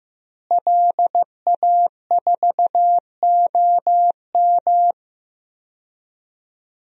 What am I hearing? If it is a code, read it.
Morse code, heard as a keyed tone, about 15 wpm, 700 Hz. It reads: LA4OM
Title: DXCC-100 Call Signs - 5 Characters - Repeated 2x - 15wpm